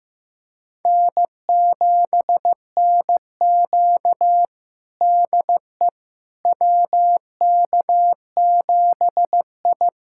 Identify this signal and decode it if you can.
Morse code, heard as a keyed tone, about 15 wpm, 700 Hz. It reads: N7NQ DE WK7I
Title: USA Callsign DE Callsign - 4 Characters - 15wpm